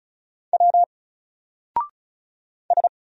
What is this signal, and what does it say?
Morse code, heard as a keyed tone, about 36 wpm, 700 Hz. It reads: W S
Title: Single Character (Letter:Number:Punctuation:Prosign) Alt 35wpm